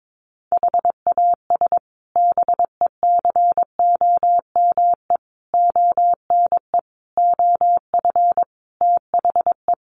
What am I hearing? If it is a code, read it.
Morse code, heard as a keyed tone, about 22 wpm, 700 Hz. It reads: HAS BECOME ONE OF THE
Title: Sets of 5 Words Encore 22wpm